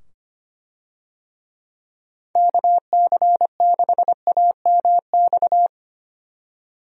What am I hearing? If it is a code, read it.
Morse code, heard as a keyed tone, about 25 wpm, 700 Hz. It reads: KC6AMX
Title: USA Call Signs - 6 Characters - 25wpm